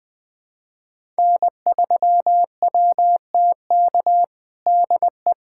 Morse code, heard as a keyed tone, about 20 wpm, 700 Hz. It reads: N3WTK DE
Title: USA Callsign DE Callsign 20wpm